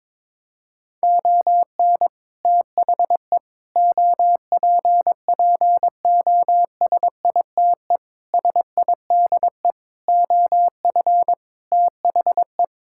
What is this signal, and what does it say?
Morse code, heard as a keyed tone, about 22 wpm, 700 Hz. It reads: ON THE OPPOSITE SIDE OF THE
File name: practice / 22wpm / Sets of 6 Words - 22wpm